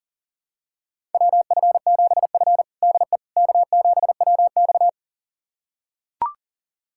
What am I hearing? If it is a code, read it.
Morse code, heard as a keyed tone, about 40 wpm, 700 Hz. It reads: WF7F DE K7WX
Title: USA Callsign DE Callsign - 4 Characters - 40wpm